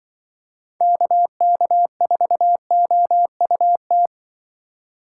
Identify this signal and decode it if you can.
Morse code, heard as a keyed tone, about 24 wpm, 700 Hz. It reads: KK4OUT
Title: Speed Racing - USA 2x3 Call Signs 20wpm